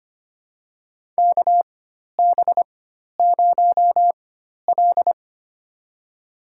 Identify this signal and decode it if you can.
Morse code, heard as a keyed tone, about 25 wpm, spaced as for 12 wpm, 700 Hz. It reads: KB0L
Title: USA Call Signs - 4 Characters - 4x Character Spacing - 25wpm